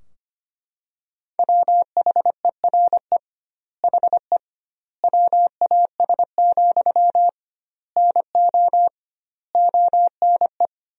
Morse code, heard as a keyed tone, about 25 wpm, 700 Hz. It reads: WHERE HE WAS, NO ONE
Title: 2x Word-Spacing - Sentences from Top 400 Words - 25wpm